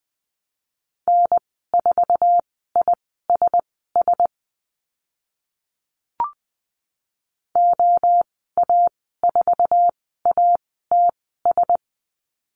Morse code, heard as a keyed tone, about 20 wpm, spaced as for 15 wpm, 700 Hz. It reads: N4ISS OA4ATS
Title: 062 Lesson 11g - Letter h - Call Signs 20wpm